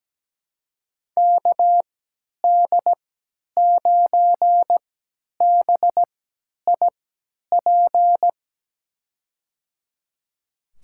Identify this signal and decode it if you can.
Morse code, heard as a keyed tone, about 17 wpm, spaced as for 10 wpm, 700 Hz. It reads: KD9BIP
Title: USA Call Signs - 6 Characters - 3x Character Spacing - 17wpm